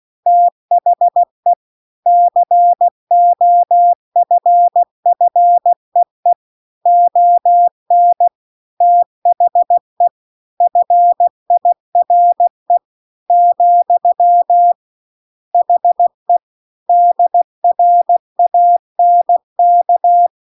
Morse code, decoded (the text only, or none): THE COFFEE ON THE FIRE , HE DRANK